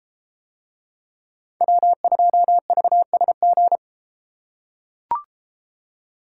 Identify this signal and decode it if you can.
Morse code, heard as a keyed tone, about 31 wpm, 700 Hz. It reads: W2VSG